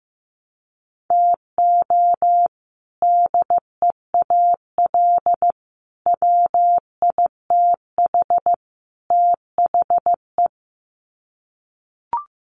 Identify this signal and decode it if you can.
Morse code, heard as a keyed tone, about 15 wpm, 700 Hz. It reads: TO DEAL WITH THE
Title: Sets of 4 Words 15wpm